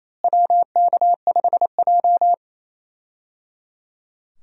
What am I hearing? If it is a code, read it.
Morse code, heard as a keyed tone, about 28 wpm, 700 Hz. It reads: WK5J
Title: USA Call Signs - 4 Characters - Repeated 3x - 28wpm